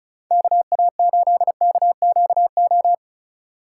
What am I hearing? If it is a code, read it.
Morse code, heard as a keyed tone, about 34 wpm, 700 Hz. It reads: KA8KQO
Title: USA Call Signs - 2x Character Spacing - 35wpm